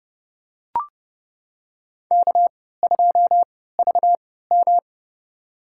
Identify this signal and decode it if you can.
Morse code, heard as a keyed tone, about 30 wpm, spaced as for 17 wpm, 700 Hz. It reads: K2VM
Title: USA Call Signs - 3x Character Spacing - 30wpm